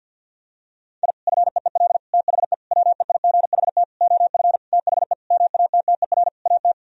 Morse code, heard as a keyed tone, about 50 wpm, 700 Hz. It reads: I FEEL THE WEIGHT OF THE MATTER AT
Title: Sentences from Top 1000 Words - Review Part 3 - 50wpm